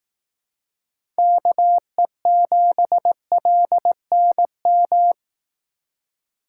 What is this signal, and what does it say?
Morse code, heard as a keyed tone, about 18 wpm, 700 Hz. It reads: KE7LNM
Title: Speed Racing - USA 2x3 Call Signs 15wpm